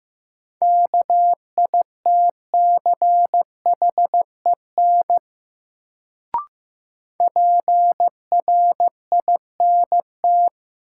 Morse code, heard as a keyed tone, about 15 wpm, 700 Hz. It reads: KITCHEN PRINT
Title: Sentences from Top 1400 Words - New Word Drill Alt - 15wpm